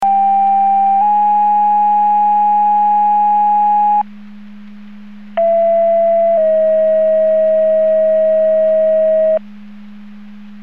Two-tone page, heard as a one-second tone, then a three-second tone, about 790 then 830 Hz, about 670 then 640 Hz. It is heard from 0.0 to 9.4 s.
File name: Portage 154.385 Paging